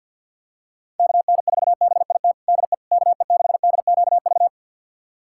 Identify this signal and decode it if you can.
Morse code, heard as a keyed tone, about 52 wpm, 700 Hz. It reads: KN4BIT DE KE6DQV
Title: USA Callsign DE Callsign - 6 Characters - 50wpm